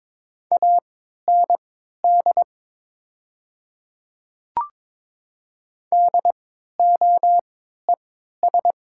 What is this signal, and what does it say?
Morse code, heard as a keyed tone, about 21 wpm, spaced as for 12 wpm, 700 Hz. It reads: AND DOES
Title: 3x Character Spacing - Top 200 Words - 22wpm